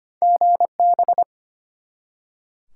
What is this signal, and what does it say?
Morse code, heard as a keyed tone, about 25 wpm, 700 Hz. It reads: GB
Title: Top 90 QSO Elements 25wpm